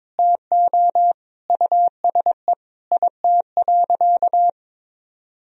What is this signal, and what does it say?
Morse code, heard as a keyed tone, about 22 wpm, 700 Hz. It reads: TO USE IT.
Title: Sentences from Top 500 Words and Abbreviations 22wpm